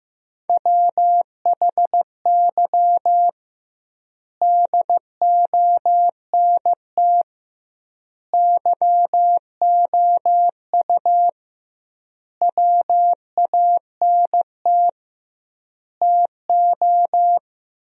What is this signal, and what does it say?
Morse code, heard as a keyed tone, about 15 wpm, 700 Hz. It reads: WHY DONT YOU WANT TO
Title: 2x Word-Spacing - Sentences from Top 300 Words - 15wpm